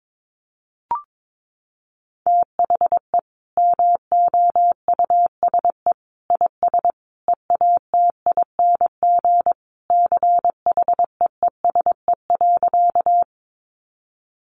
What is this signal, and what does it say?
Morse code, heard as a keyed tone, about 22 wpm, 700 Hz. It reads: THE MOUSE IS EATING CHEESE.
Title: Easy Phrases 22wpm